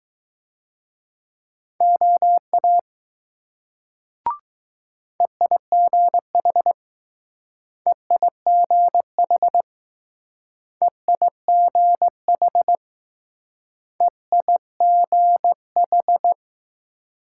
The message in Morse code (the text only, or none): OA EIGH EIGH EIGH EIGH